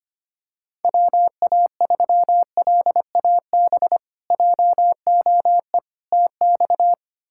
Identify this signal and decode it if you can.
Morse code, heard as a keyed tone, about 25 wpm, 700 Hz. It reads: WA3LAB JOE TX
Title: K1USN SST QSO Practice - 25wpm